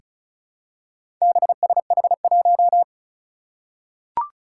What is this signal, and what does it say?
Morse code, heard as a keyed tone, about 36 wpm, 700 Hz. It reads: BSH1
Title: Four Challenging Letter-Number Combinations 35wpm